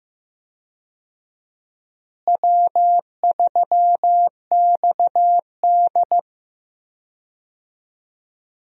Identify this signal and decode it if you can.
Morse code, heard as a keyed tone, about 15 wpm, 700 Hz. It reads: W3XD
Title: USA Call Signs - 4 Characters - Repeated 3x - 15wpm